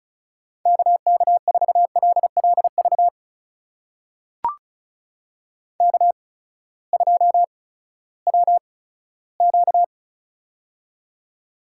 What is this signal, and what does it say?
Morse code, heard as a keyed tone, about 36 wpm, spaced as for 18 wpm, 700 Hz. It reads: KK4LLV K2WQ
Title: USA Call Signs - 8x Character Spacing - 35wpm